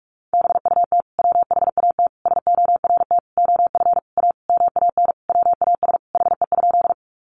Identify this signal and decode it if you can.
Morse code, heard as a keyed tone, about 44 wpm, 700 Hz. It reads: BUT WHAT SORT OF A MAN WAS HE?
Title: Sentences from Top 1300 Words 45wpm